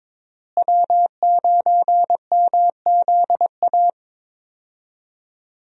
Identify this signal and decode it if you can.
Morse code, heard as a keyed tone, about 22 wpm, 700 Hz. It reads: W9MZA
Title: USA Call Signs - 5 Characters - Repeated 2x - 22wpm